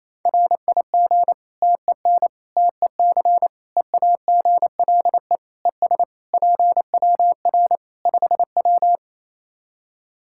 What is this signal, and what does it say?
Morse code, heard as a keyed tone, about 27 wpm, 700 Hz. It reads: RIG TEN TEC EAGLE ES PWR 5W
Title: Practice QSO - Exchange 2 - Info - 28wpm